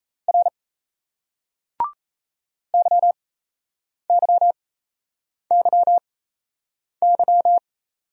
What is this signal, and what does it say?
Morse code, heard as a keyed tone, about 38 wpm, spaced as for 8 wpm, 700 Hz. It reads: R YYYY